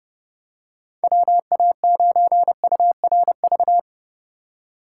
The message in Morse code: WA9URV

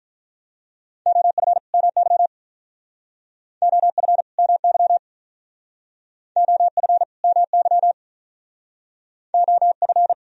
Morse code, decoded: OF MY OF MY OF MY OF

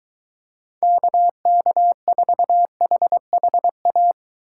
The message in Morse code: KK4HHA

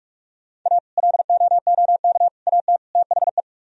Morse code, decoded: A LOOK AT THE

